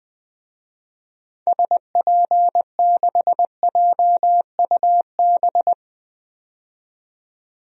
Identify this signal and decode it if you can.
Morse code, heard as a keyed tone, about 20 wpm, 700 Hz. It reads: SP6JUB
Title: DXCC-100 Call Signs - 6 Characters - 20wpm